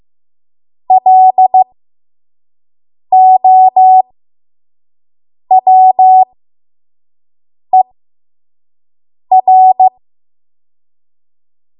Morse code, heard as a keyed tone, about 15 wpm, spaced as for 5 wpm, 750 Hz. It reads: LOWER
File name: Morse / 05WPM